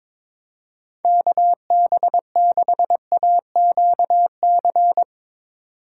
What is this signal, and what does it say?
Morse code, heard as a keyed tone, about 22 wpm, 700 Hz. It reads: KB6AQC
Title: USA Call Signs - 6 Characters - 4x Character Spacing - 22wpm